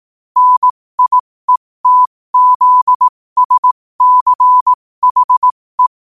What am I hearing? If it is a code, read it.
Morse code, heard as a keyed tone, about 18 wpm, spaced as for 15 wpm, 1000 Hz. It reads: NIETZSCHE
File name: qotdfiles / 15wpm